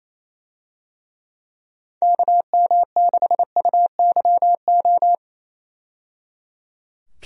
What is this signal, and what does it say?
Morse code, heard as a keyed tone, about 28 wpm, 700 Hz. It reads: KM6UYO
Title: USA Call Signs - 6 Characters - 28wpm